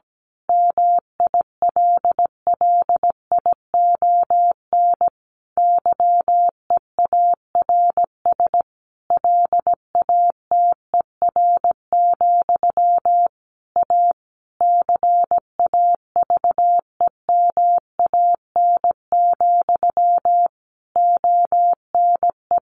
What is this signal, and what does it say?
Morse code, heard as a keyed tone, about 17 wpm, 700 Hz. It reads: MILLION YEARS LATER, A CAVEMAN, ONE
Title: A Sound Of Thunder 17wpm